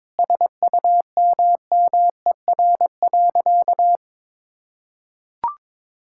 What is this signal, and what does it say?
Morse code, heard as a keyed tone, about 22 wpm, 700 Hz. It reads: SUMMER.
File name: practice / 22wpm / Sentences with 4-Word Phrases - 22wpm